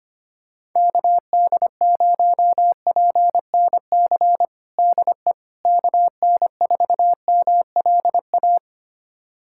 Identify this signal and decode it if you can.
Morse code, heard as a keyed tone, about 25 wpm, 700 Hz. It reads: KD0PNC DE KN4MLA
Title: USA Callsign DE Callsign - 6 Characters - 25wpm